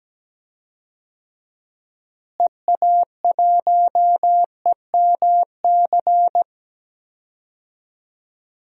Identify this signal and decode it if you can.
Morse code, heard as a keyed tone, about 17 wpm, 700 Hz. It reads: EA1EMC
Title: DXCC-100 Call Signs - 6 Characters - Repeated 3x - 17wpm